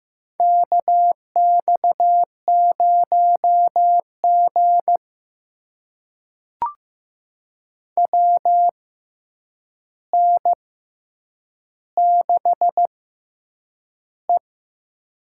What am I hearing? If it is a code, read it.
Morse code, heard as a keyed tone, about 15 wpm, spaced as for 8 wpm, 700 Hz. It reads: KX0G WN6E
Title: USA Call Signs - 4 Characters - 6x Character Spacing - 15wpm